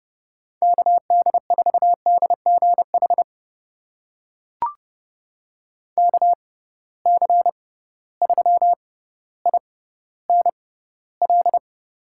Morse code, heard as a keyed tone, about 30 wpm, spaced as for 15 wpm, 700 Hz. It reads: KD4DGH KC3INL
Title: USA Call Signs - 6 Characters - 6x Character Spacing - 30wpm